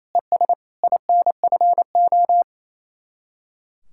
Morse code, heard as a keyed tone, about 27 wpm, 700 Hz. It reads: ES INFO
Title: Speed-Racing Common QSO Phrases - 28wpm